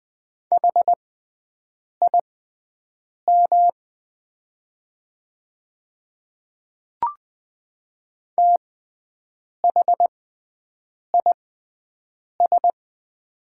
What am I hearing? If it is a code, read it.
Morse code, heard as a keyed tone, about 20 wpm, spaced as for 7 wpm, 700 Hz. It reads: HIM THIS
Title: 6x Character Spacing - Top 100 Words - 20wpm